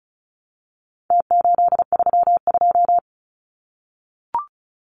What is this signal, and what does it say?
Morse code, heard as a keyed tone, about 34 wpm, 700 Hz. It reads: T832
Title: Numbers - Contest Style - 4-Digit 35wpm